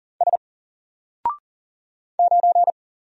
Morse code, heard as a keyed tone, about 40 wpm, 700 Hz. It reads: S 9
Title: Single Letter-Number Alt 40wpm